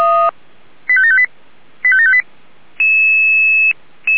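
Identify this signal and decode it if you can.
Five-tone selective call (ZVEI), heard as quick runs of five tones, tones of about 70 ms. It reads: 85758 85758
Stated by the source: ZVEI five-tone call received by SDR.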